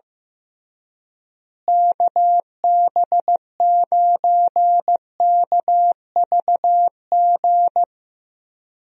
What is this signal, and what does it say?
Morse code, heard as a keyed tone, about 15 wpm, 700 Hz. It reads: KB9KVG
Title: USA Call Signs - 6 Characters - Repeated 3x - 15wpm